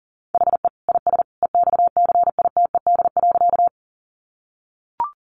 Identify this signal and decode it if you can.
Morse code, heard as a keyed tone, about 40 wpm, 700 Hz. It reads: HE IS EXCITED.
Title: Additional Easy Phrases 40wpm